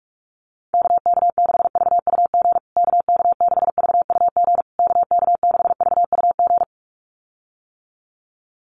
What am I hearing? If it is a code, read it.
Morse code, heard as a keyed tone, about 44 wpm, 700 Hz. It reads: KK6VUG KK6VUG KK6VUG
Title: USA Call Signs Repeated 3x 45wpm